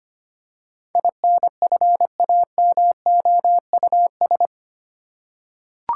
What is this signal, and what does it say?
Morse code, heard as a keyed tone, about 25 wpm, 700 Hz. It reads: INFAMOUS
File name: Eight Letter Words - 25wpm